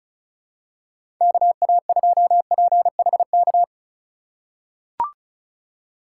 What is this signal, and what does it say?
Morse code, heard as a keyed tone, about 36 wpm, 700 Hz. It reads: KA2PHK
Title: USA Call Signs - 6 Characters - 3x Character Spacing - 35wpm